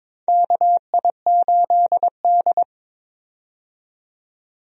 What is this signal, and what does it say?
Morse code, heard as a keyed tone, about 23 wpm, 700 Hz. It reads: KI8D
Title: USA Call Signs - 4 Characters - 22wpm